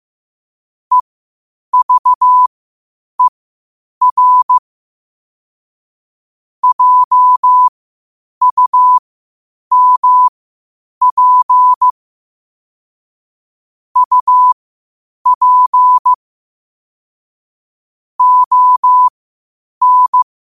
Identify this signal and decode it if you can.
Morse code, heard as a keyed tone, about 15 wpm, spaced as for 8 wpm, 1000 Hz. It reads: EVER JUMP UP ON
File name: qotdfiles / 8wpm